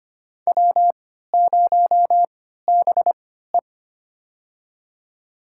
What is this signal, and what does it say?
Morse code, heard as a keyed tone, about 25 wpm, spaced as for 14 wpm, 700 Hz. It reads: W0BE
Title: USA Call Signs - 4 Characters - 3x Character Spacing - 25wpm